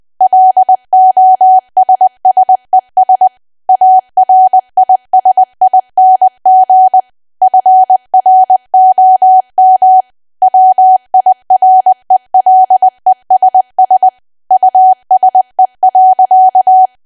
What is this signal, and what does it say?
Morse code, heard as a keyed tone, about 20 wpm, 750 Hz. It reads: LOSSES ARISING FROM WIRELESS USE.
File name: cwa / W1AW-Level3 / 20WPM